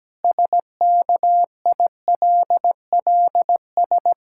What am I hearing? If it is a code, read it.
Morse code, heard as a keyed tone, about 17 wpm, 700 Hz. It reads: SKILLS
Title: Pangram Challenge v2 - 17wpm